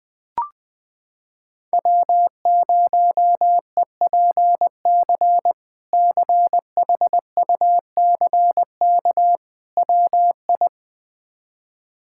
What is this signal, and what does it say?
Morse code, heard as a keyed tone, about 20 wpm, 700 Hz. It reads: W0EPC CHUCK WI
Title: K1USN SST QSO Practice - 20wpm